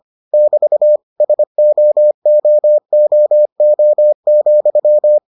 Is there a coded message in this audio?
Yes: Morse code from 0.3 to 5.2 s, heard as a keyed tone, about 25 wpm, 600 Hz. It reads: <BT> SOOOO,